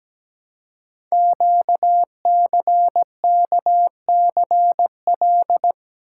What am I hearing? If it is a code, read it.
Morse code, heard as a keyed tone, about 17 wpm, 700 Hz. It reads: QCKCL
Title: Code Groups - 5 Letters - 17wpm